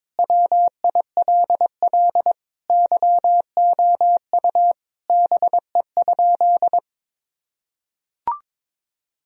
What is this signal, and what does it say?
Morse code, heard as a keyed tone, about 22 wpm, 700 Hz. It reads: WILL YOU BE?
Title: Sentences from Top 1000 Words - Review Part 4 - 22wpm